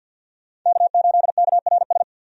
Morse code, heard as a keyed tone, about 52 wpm, 700 Hz. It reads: K8KRS
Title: USA Call Signs - 5 Characters - 50wpm